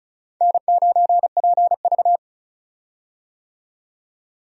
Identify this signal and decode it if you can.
Morse code, heard as a keyed tone, about 36 wpm, 700 Hz. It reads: N9PV
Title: USA Call Signs - 4 Characters - 35wpm